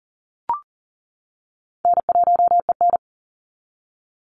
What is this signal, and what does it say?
Morse code, heard as a keyed tone, about 40 wpm, 700 Hz. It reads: N1EN
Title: Easy Call Signs - 40wpm